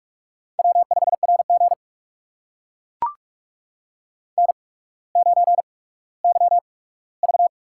Morse code, heard as a keyed tone, about 44 wpm, spaced as for 19 wpm, 700 Hz. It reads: W5RG N9YV
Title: USA Call Signs - 4 Characters - 8x Character Spacing - 45wpm